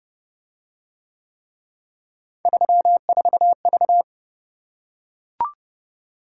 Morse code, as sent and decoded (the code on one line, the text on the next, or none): ...-- ....- ...-
34V